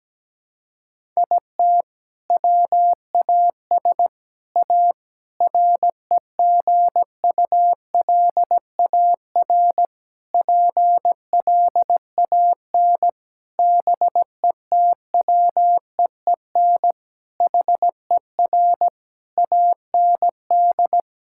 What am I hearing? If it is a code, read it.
Morse code, heard as a keyed tone, about 17 wpm, 700 Hz. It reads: IT WAS A REGULAR PLAN BETWEEN HER AND
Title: Sentences from Top 1000 Words - Review Part 6 - 17wpm